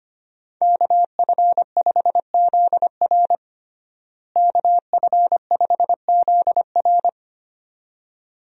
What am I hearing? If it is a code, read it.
Morse code, heard as a keyed tone, about 25 wpm, 700 Hz. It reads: KF5ZR KF5ZR
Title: USA Call Signs - 5 Characters - Repeated 2x - 25wpm